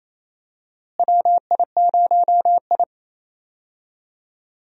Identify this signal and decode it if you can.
Morse code, heard as a keyed tone, about 28 wpm, 700 Hz. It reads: WI0I
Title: USA Call Signs - 4 Characters - Repeated 2x - 28wpm